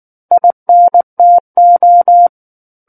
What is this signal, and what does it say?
Morse code, heard as a keyed tone, about 19 wpm, 700 Hz. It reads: INTO
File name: Book One / 10-Chapter 10 - 19 WPM